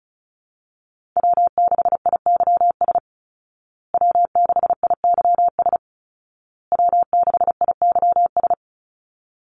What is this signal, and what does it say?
Morse code, heard as a keyed tone, about 34 wpm, 700 Hz. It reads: W6IYS W6IYS W6IYS
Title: Call Signs Repeated 2x - 35wpm